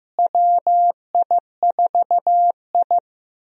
Morse code, heard as a keyed tone, about 15 wpm, 700 Hz. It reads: WI4I
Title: USA Call Signs - 4 Characters - Repeated 3x - 15wpm